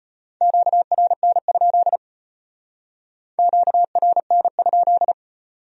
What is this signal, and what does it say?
Morse code, heard as a keyed tone, about 33 wpm, 700 Hz. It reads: QRN? QRN?